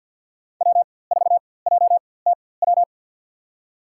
Morse code, heard as a keyed tone, about 48 wpm, spaced as for 23 wpm, 700 Hz. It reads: W4JTW
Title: USA Call Signs - 5 Characters - 4x Character Spacing - 50wpm